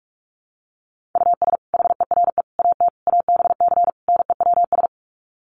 Morse code, heard as a keyed tone, about 44 wpm, 700 Hz. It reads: US HERE AT ABC NEWS
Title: Sets of 5 Words 45wpm